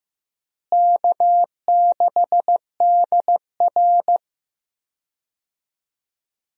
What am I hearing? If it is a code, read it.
Morse code, heard as a keyed tone, about 15 wpm, 700 Hz. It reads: K6DR
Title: USA Call Signs - 4 Characters - Repeated 2x - 15wpm